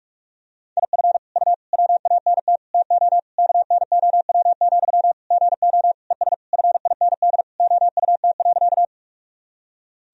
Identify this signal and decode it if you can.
Morse code, heard as a keyed tone, about 45 wpm, 700 Hz. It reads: IF U WANT TO KNOW, GO ES FIND OUT.
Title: Sentences from Top 200 Words and Abbreviations 45wpm